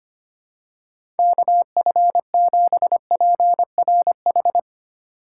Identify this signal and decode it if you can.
Morse code, heard as a keyed tone, about 25 wpm, 700 Hz. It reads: KF7PRH
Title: USA Call Signs - 2x Character Spacing - 25wpm